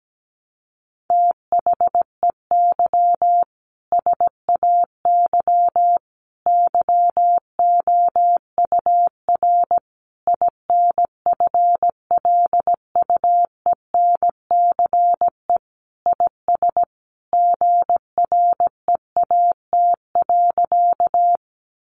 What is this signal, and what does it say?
Morse code, heard as a keyed tone, about 17 wpm, 700 Hz. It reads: THEY SAY YOUR INFLUENCE IS GREAT.
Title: Sentences from Top 1100 Words 17wpm